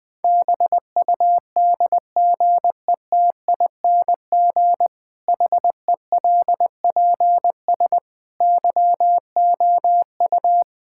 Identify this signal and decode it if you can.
Morse code, heard as a keyed tone, about 20 wpm, 700 Hz. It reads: BUDGETING HELPS YOU
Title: Difficult Sentences with Hints - 20wpm